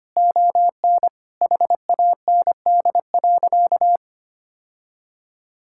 Morse code, heard as a keyed tone, about 25 wpm, 700 Hz. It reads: ON HAND.
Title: Sentences from Top 1000 Words - Review Part 5 - 25wpm